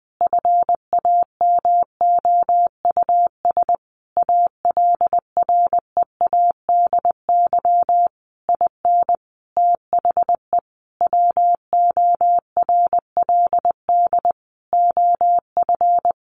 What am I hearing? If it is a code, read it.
Morse code, heard as a keyed tone, about 19 wpm, 700 Hz. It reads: FAMOUS ALREADY IN THE WORLD OF
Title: Sentences from Top 1500 Words 20wpm